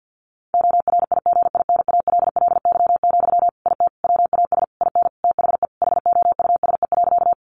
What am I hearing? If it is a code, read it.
Morse code, heard as a keyed tone, about 50 wpm, 700 Hz. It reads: ORIGINALLY, IT WAS IN THE HOUSE.
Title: Sentences from Top 1500 Words 50wpm